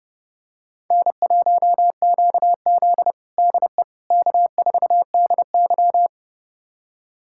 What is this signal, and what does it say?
Morse code, heard as a keyed tone, about 30 wpm, 700 Hz. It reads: N1QZ DE K4DY
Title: USA Callsign DE Callsign - 4 Characters - 30wpm